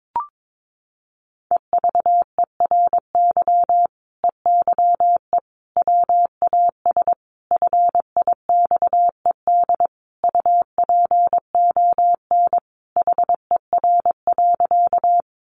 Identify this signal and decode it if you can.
Morse code, heard as a keyed tone, about 22 wpm, 700 Hz. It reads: EVERY EYE WAS FIXED UPON HER.
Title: Sentences from Top 1100 Words 22wpm